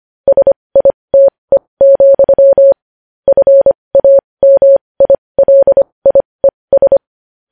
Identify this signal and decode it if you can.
Morse code, heard as a keyed tone, about 25 wpm, spaced as for 20 wpm, 550 Hz. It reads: SITE, FAMILIES
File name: podcast / News headlines-1x-25x20